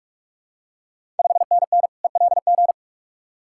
Morse code, heard as a keyed tone, about 45 wpm, 700 Hz. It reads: 5NN ELG
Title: Ontario QSO Party Exchange 45wpm